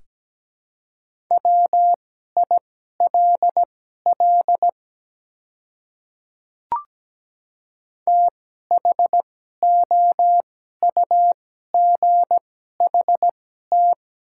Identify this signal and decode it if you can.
Morse code, heard as a keyed tone, about 17 wpm, spaced as for 12 wpm, 700 Hz. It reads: WILL THOUGHT
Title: 2x Character Spacing - Top 300 Words - 17wpm